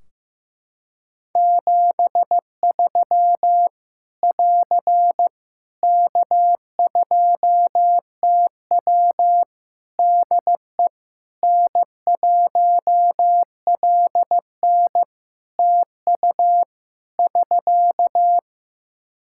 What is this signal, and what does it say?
Morse code, heard as a keyed tone, about 14 wpm, 700 Hz. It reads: 73 <AR> K2TW DE N1LN TU <SK>
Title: Practice QSO - Exchange 4 - Ending - 15wpm